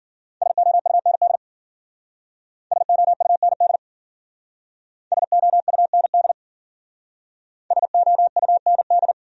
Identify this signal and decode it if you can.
Morse code, heard as a keyed tone, about 51 wpm, 700 Hz. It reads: SOUND SOUND SOUND SOUND